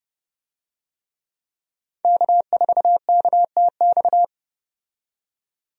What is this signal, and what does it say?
Morse code, heard as a keyed tone, about 31 wpm, 700 Hz. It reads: K4KTX
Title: USA Call Signs - 5 Characters - 30wpm